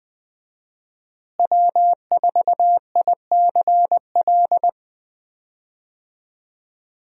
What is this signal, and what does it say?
Morse code, heard as a keyed tone, about 20 wpm, 700 Hz. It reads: W4ICL
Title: USA Call Signs - 5 Characters - Repeated 3x - 20wpm